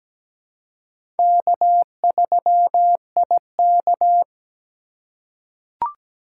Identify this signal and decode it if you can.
Morse code, heard as a keyed tone, about 17 wpm, 700 Hz. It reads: K3IK
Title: USA Call Signs - 4 Characters - 4x Character Spacing - 17wpm